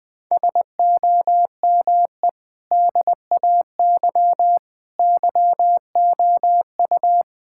Morse code, heard as a keyed tone, about 20 wpm, 700 Hz. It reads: SOME DAY YOU
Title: Sentences from Top 1000 Words - Review Part 6 - 20wpm